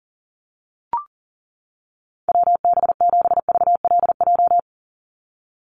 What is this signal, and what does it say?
Morse code, heard as a keyed tone, about 40 wpm, 700 Hz. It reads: WB7VLJ
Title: USA Call Signs 40wpm